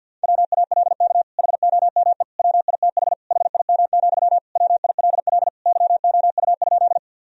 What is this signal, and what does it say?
Morse code, heard as a keyed tone, about 49 wpm, 700 Hz. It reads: WALK HOME WITH HIM, WILL YOU?
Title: Sentences from Top 1000 Words - Review Part 1 - 50wpm